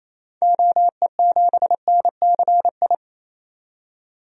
Morse code, heard as a keyed tone, about 28 wpm, 700 Hz. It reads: OE7NCI
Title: DXCC-100 Call Signs - 6 Characters - 28wpm